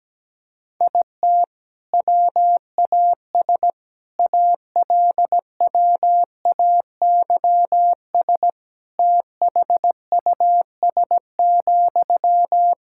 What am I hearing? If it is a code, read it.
Morse code, heard as a keyed tone, about 17 wpm, 700 Hz. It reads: IT WAS ALWAYS THUS,
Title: Sentences from Top 1000 Words - Review Part 5 - 17wpm